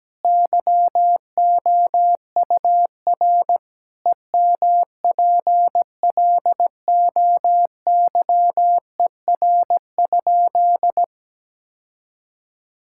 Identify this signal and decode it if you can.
Morse code, heard as a keyed tone, about 17 wpm, 700 Hz. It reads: YOUR EMPLOYER?
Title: Sentences from Top 1600 Words 17wpm